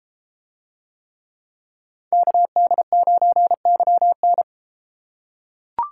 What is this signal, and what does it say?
Morse code, heard as a keyed tone, about 30 wpm, 700 Hz. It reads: KD9YN